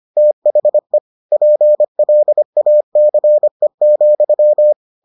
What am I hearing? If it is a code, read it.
Morse code, heard as a keyed tone, about 25 wpm, 600 Hz. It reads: THE PLACE,